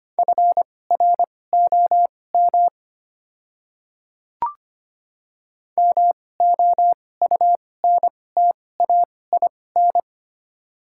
Morse code, heard as a keyed tone, about 25 wpm, spaced as for 18 wpm, 690 Hz. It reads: FROM MOUNTAIN
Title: 2x Character Spacing - Top 500 Words - 25wpm